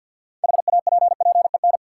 Morse code, heard as a keyed tone, about 52 wpm, 700 Hz. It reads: HAPPEN
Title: Six Letter Words - 50wpm